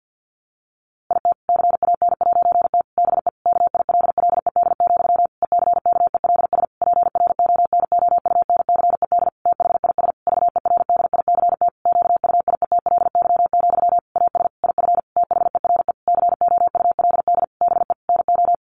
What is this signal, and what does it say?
Morse code, heard as a keyed tone, about 50 wpm, 700 Hz. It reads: IT CAN'T BE KILLED, ECKELS PRONOUNCED THIS VERDICT QUIETLY, AS IF THERE COULD BE NO
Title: A Sound Of Thunder 50wpm